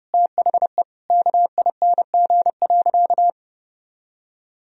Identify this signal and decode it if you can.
Morse code, heard as a keyed tone, about 30 wpm, 700 Hz. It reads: THE KING.
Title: Sentences from Top 1000 Words - Review Part 2 - 30wpm